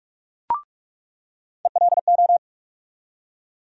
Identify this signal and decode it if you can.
Morse code, heard as a keyed tone, about 42 wpm, 700 Hz. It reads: ELO
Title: USA Call Sign Postfixes - 3 Letters 45wpm